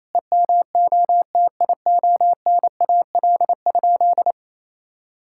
Morse code, heard as a keyed tone, about 28 wpm, 700 Hz. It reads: EMOTIONAL?
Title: Sentences from Top 1000 Words - Review Part 5 - 28wpm